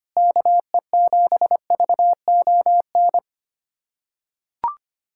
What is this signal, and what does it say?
Morse code, heard as a keyed tone, about 25 wpm, 700 Hz. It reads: KE7VON
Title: USA Callsign DE Callsign - 6 Characters - 25wpm